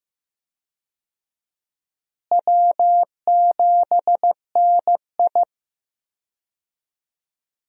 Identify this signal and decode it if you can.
Morse code, heard as a keyed tone, about 15 wpm, 700 Hz. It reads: W7NI
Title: USA Call Signs - 4 Characters - Repeated 2x - 15wpm